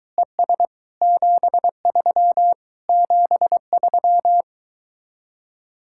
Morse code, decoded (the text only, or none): ES 73 73